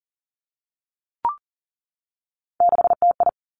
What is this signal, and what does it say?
Morse code, heard as a keyed tone, about 38 wpm, 700 Hz. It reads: 6TI
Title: Three Challenging Letter-Number Combinations 40wpm